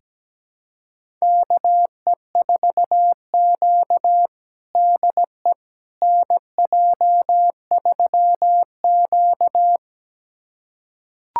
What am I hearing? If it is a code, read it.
Morse code, heard as a keyed tone, about 17 wpm, 700 Hz. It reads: KE4Q DE NJ3Q
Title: USA Callsign DE Callsign - 4 Characters - 17wpm